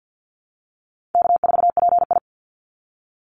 Morse code, heard as a keyed tone, about 50 wpm, 700 Hz. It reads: K4PI
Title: Easy Call Signs Repeated 3x - 50wpm